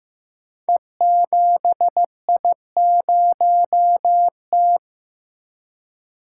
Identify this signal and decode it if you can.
Morse code, heard as a keyed tone, about 15 wpm, 700 Hz. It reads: E7I0T
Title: Code Groups - 5 Letter:Numbers - 15wpm